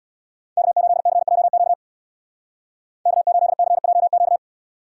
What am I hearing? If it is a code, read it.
Morse code, heard as a keyed tone, about 62 wpm, 700 Hz. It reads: K8CJQ K8CJQ
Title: Speed Racing - USA Call Signs 50wpm